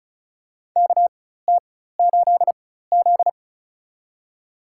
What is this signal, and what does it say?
Morse code, heard as a keyed tone, about 34 wpm, spaced as for 16 wpm, 700 Hz. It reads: KT8Z
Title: USA Call Signs - 4 Characters - 4x Character Spacing - 35wpm